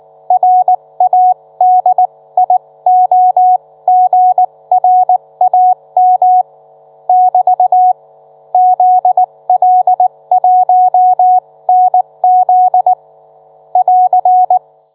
Morse code, heard as a keyed tone, about 19 wpm, spaced as for 16 wpm, 730 Hz. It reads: RADIOGRAM <BT> ZL1NZ <AR>